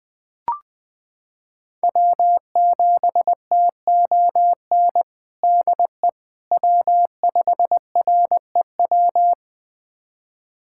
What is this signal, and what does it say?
Morse code, heard as a keyed tone, about 20 wpm, 700 Hz. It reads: W7TON DE W5REW
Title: USA Callsign DE Callsign - 5 Characters - 20wpm